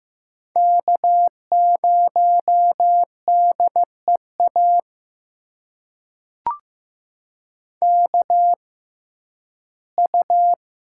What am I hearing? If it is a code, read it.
Morse code, heard as a keyed tone, about 15 wpm, spaced as for 11 wpm, 700 Hz. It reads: K0DEA KU
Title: USA Call Signs - 5 Characters - 6x Character Spacing - 15wpm